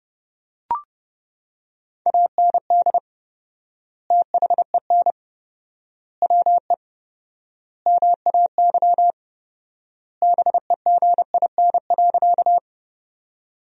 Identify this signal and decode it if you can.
Morse code, heard as a keyed tone, about 32 wpm, 700 Hz. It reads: AND THEN WE MAY BEGIN.
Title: 4x Word-Spacing - Sentences from Top 300 Words - 30wpm